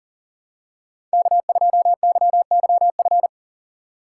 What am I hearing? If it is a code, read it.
Morse code, heard as a keyed tone, about 37 wpm, 700 Hz. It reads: K2YYF
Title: USA Call Signs - 4x Character Spacing - 40wpm